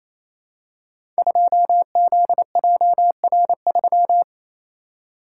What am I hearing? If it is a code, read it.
Morse code, heard as a keyed tone, about 27 wpm, 700 Hz. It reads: 2ZJR3
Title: Code Groups - 5 Letter:Numbers - 28wpm